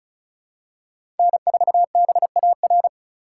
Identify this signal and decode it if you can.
Morse code, heard as a keyed tone, about 34 wpm, 700 Hz. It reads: N4BAR
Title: USA Call Signs - 5 Characters - 2x Character Spacing - 35wpm